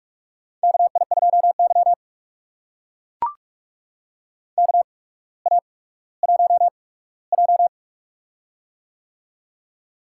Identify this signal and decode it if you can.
Morse code, heard as a keyed tone, about 44 wpm, spaced as for 19 wpm, 700 Hz. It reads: KI2Y KA1J
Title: USA Call Signs - 4 Characters - 8x Character Spacing - 45wpm